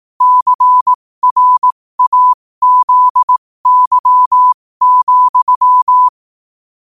Morse code, heard as a keyed tone, about 18 wpm, spaced as for 15 wpm, 1000 Hz. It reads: CRAZY,
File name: qotdfiles / 15wpm